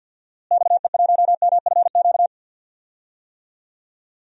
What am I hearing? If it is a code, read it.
Morse code, heard as a keyed tone, about 47 wpm, 700 Hz. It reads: XE1MFQ
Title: DXCC-100 Call Signs - 6 Characters - Repeated 2x - 50wpm